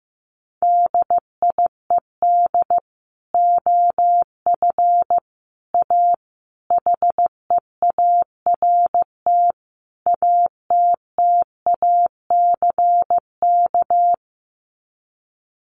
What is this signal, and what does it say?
Morse code, heard as a keyed tone, about 15 wpm, 700 Hz. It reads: DIED OF A HEART ATTACK
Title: Sets of 5 Words Encore2 15wpm